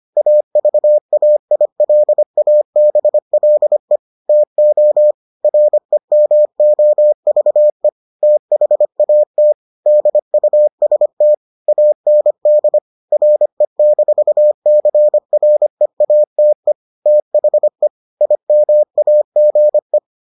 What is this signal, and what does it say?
Morse code, heard as a keyed tone, about 25 wpm, 600 Hz. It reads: AVAILABLE TO REMOVE THAT DUST AND RE-CREATE THE IMAGE